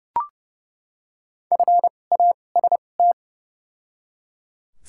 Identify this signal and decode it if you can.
Morse code, heard as a keyed tone, about 30 wpm, spaced as for 22 wpm, 700 Hz. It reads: FAST
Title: 2x Character Spacing - Top 500 Words - 30wpm